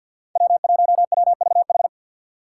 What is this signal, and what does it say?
Morse code, heard as a keyed tone, about 48 wpm, 700 Hz. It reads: W1WVH
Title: Five Challenging Letter-Number Combinations 50wpm